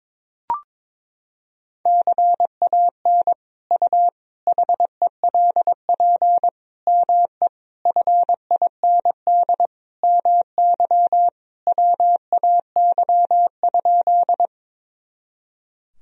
Morse code, heard as a keyed tone, about 22 wpm, 700 Hz. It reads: CAN U HELP ME FIND MY WAY?
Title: Sentences from Top 200 Words and Abbreviations 22wpm